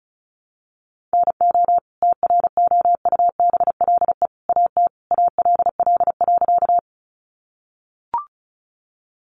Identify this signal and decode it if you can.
Morse code, heard as a keyed tone, about 34 wpm, 700 Hz. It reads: NO TROUBLE AT ALL.
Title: Sentences from Top 1200 Words 35wpm